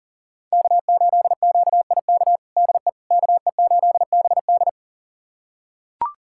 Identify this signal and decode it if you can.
Morse code, heard as a keyed tone, about 40 wpm, 700 Hz. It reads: K8QIK DE KE8BD
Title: USA Callsign DE Callsign - 5 Characters - 40wpm